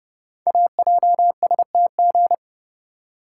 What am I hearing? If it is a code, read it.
Morse code, heard as a keyed tone, about 30 wpm, 690 Hz. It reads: AJSTG
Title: Code Groups - 5 Letters - 30wpm